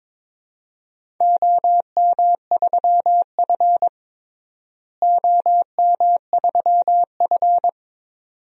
Morse code, heard as a keyed tone, about 22 wpm, 700 Hz. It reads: OM3F OM3F
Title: DXCC-100 Call Signs - 4 Characters - Repeated 2x - 22wpm